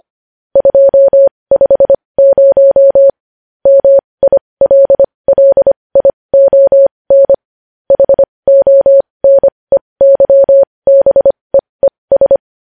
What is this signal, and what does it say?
Morse code, heard as a keyed tone, about 25 wpm, spaced as for 20 wpm, 550 Hz. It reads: 250 MILLION HONEYBEES
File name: podcast / News headlines-1x-25x20